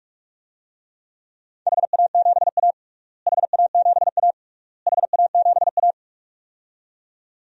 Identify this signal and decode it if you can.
Morse code, heard as a keyed tone, about 46 wpm, 690 Hz. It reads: HA7A HA7A HA7A
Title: DXCC-100 Call Signs - 4 Characters - Repeated 3x - 45wpm